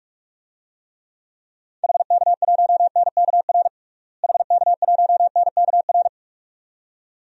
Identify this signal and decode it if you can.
Morse code, heard as a keyed tone, about 41 wpm, 700 Hz. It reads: HK1NKR HK1NKR
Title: DXCC-100 Call Signs - 6 Characters - Repeated 2x - 45wpm